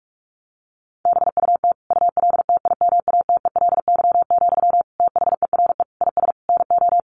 Morse code, heard as a keyed tone, about 44 wpm, 700 Hz. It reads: BUT ULTIMATELY, THERE IS NO
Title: Sentences from Top 1200 Words 45wpm